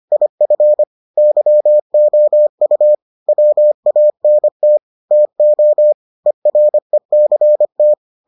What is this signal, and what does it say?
Morse code, heard as a keyed tone, about 25 wpm, 600 Hz. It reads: IF YOU WANT TO ERECT